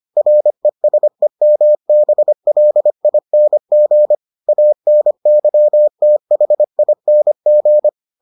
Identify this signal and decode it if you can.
Morse code, heard as a keyed tone, about 25 wpm, 600 Hz. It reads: RESEMBLING ANYTHING